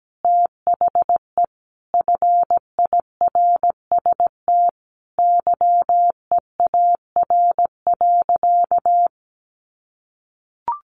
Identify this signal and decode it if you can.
Morse code, heard as a keyed tone, about 17 wpm, 700 Hz. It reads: THE FIRST YEAR.
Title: Sentences from Top 1300 Words 17wpm